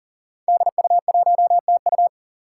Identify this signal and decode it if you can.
Morse code, heard as a keyed tone, about 40 wpm, 700 Hz. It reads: DU1TU
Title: Five Challenging Letter-Number Combinations 40wpm